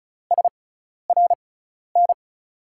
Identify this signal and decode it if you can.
Morse code, heard as a keyed tone, about 36 wpm, spaced as for 12 wpm, 700 Hz. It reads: SRN
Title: USA Call Signs - 6 Characters - 6x Character Spacing - 35wpm